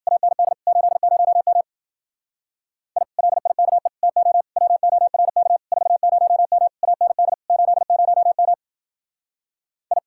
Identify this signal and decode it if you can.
Morse code, heard as a keyed tone, about 49 wpm, 700 Hz. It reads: AND 80M I LIKE TO WORK 40M AND 80M I
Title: Speed-Racing Common QSO Phrases Encore - 40wpm